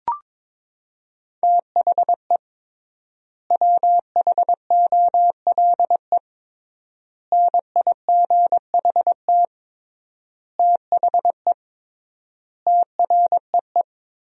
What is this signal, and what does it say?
Morse code, heard as a keyed tone, about 21 wpm, 700 Hz. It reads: THE WHOLE NIGHT THE TREE
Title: 3x Word-Spacing - Sentences from Top 500 Words - 22wpm